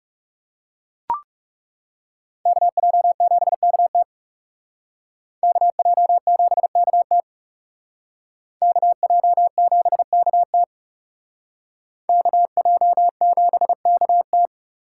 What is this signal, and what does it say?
Morse code, heard as a keyed tone, about 34 wpm, 700 Hz. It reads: KJ7KT KJ7KT KJ7KT KJ7KT
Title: Speed Racing - USA 2x2 Call Signs 30wpm